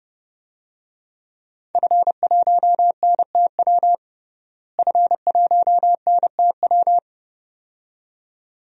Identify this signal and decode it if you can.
Morse code, heard as a keyed tone, about 30 wpm, 700 Hz. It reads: F1NTW F1NTW
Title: DXCC-100 Call Signs - 5 Characters - Repeated 2x - 30wpm